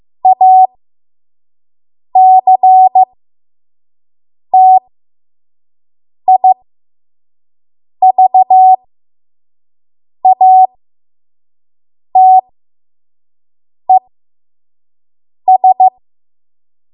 Morse code, heard as a keyed tone, about 15 wpm, spaced as for 5 wpm, 750 Hz. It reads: ACTIVATES
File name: Morse / 05WPM